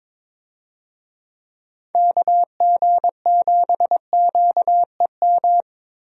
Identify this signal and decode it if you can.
Morse code, heard as a keyed tone, about 22 wpm, 700 Hz. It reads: KG7QEM
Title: USA Call Signs - 6 Characters - Repeated 3x - 22wpm